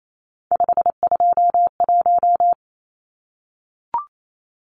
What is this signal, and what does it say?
Morse code, heard as a keyed tone, about 28 wpm, 700 Hz. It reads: 521
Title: Numbers - 3-Digit 28wpm